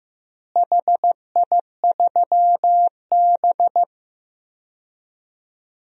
Morse code, heard as a keyed tone, about 15 wpm, 700 Hz. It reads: HI3B
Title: DXCC-100 Call Signs - 4 Characters - Repeated 2x - 15wpm